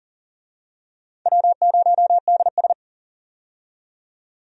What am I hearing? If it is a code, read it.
Morse code, heard as a keyed tone, about 40 wpm, 690 Hz. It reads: W0DS
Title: USA Call Signs - 4 Characters - 40wpm